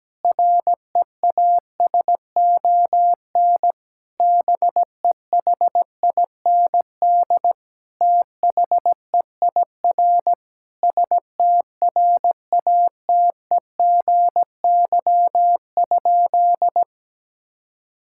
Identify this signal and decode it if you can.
Morse code, heard as a keyed tone, about 17 wpm, 700 Hz. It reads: REASON BEHIND THEIR STRATEGY?
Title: Sentences from Top 1000 Words - Review Part 5 - 17wpm